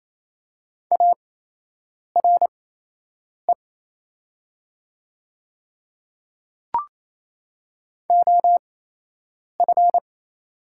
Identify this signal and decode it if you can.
Morse code, heard as a keyed tone, about 28 wpm, spaced as for 8 wpm, 700 Hz. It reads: ARE OF
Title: 8x Character Spacing - Top 100 Words - 28wpm